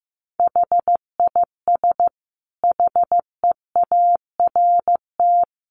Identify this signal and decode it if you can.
Morse code, heard as a keyed tone, about 15 wpm, 700 Hz. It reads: HIS HEART
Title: Sentences from Top 1300 Words 15wpm